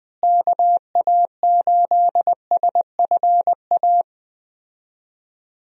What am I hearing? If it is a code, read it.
Morse code, heard as a keyed tone, about 21 wpm, 700 Hz. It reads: KA8SFA
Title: USA Call Signs - 6 Characters - Repeated 2x - 20wpm